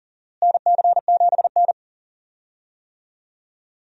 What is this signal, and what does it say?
Morse code, heard as a keyed tone, about 40 wpm, 700 Hz. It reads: NC7N
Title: USA Call Signs - 4 Characters - Repeated 3x - 40wpm